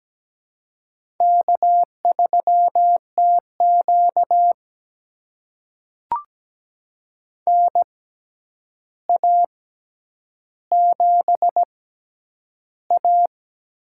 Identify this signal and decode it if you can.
Morse code, heard as a keyed tone, about 17 wpm, spaced as for 9 wpm, 700 Hz. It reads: K3TQ NA7A
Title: USA Call Signs - 4 Characters - 6x Character Spacing - 17wpm